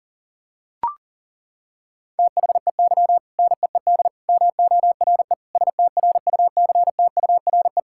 Morse code, heard as a keyed tone, about 37 wpm, 700 Hz. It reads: THEY NEED MORE STRUCTURE
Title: Sentences from Top 1000 Words - Review Part 6 - 40wpm